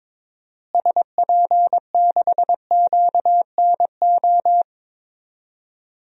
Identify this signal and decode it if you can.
Morse code, heard as a keyed tone, about 22 wpm, 700 Hz. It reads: SP6QNO
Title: DXCC-100 Call Signs - 6 Characters - Repeated 3x - 22wpm